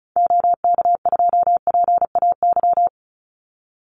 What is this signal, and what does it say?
Morse code, heard as a keyed tone, about 36 wpm, 700 Hz. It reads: OK2PAY
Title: Call Signs - 35wpm